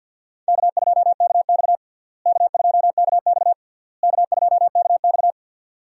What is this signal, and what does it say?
Morse code, heard as a keyed tone, about 46 wpm, 700 Hz. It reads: K2KX K2KX K2KX
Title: USA Call Signs - 4 Characters - Repeated 3x - 50wpm